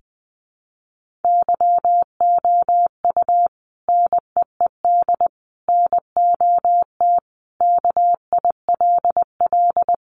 Morse code, heard as a keyed tone, about 20 wpm, 700 Hz. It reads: YOU NEED NOT KILL
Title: Sentences from Top 1400 Words 20wpm